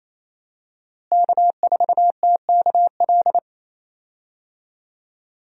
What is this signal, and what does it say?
Morse code, heard as a keyed tone, about 28 wpm, 700 Hz. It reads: K4TKL
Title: USA Call Signs - 5 Characters - Repeated 2x - 28wpm